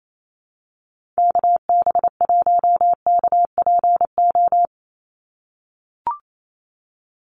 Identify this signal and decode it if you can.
Morse code, heard as a keyed tone, about 28 wpm, 700 Hz. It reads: KB1KPO
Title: USA Call Signs 28wpm